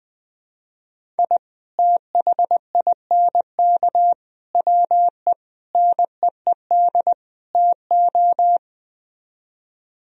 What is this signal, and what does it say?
Morse code, heard as a keyed tone, about 20 wpm, 700 Hz. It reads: I THINK WE NEED TO
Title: Speed-Racing - Sets of 5 Words 17wpm